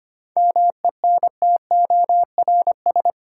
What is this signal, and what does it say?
Morse code, heard as a keyed tone, about 25 wpm, 700 Hz. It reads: MENTORS
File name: practice / 25wpm / Difficult Sentences with Hints - 25wpm